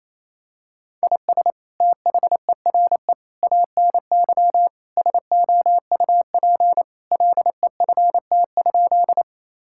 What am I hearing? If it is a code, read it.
Morse code, heard as a keyed tone, about 28 wpm, 700 Hz. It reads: IS THERE ANY SOUP LEFT?
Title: Sentences from Top 1000 Words - Review Part 6 - 28wpm